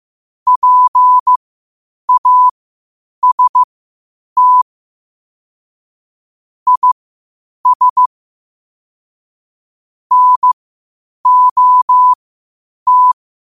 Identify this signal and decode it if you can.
Morse code, heard as a keyed tone, about 15 wpm, spaced as for 8 wpm, 1000 Hz. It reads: PAST IS NOT
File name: qotdfiles / 8wpm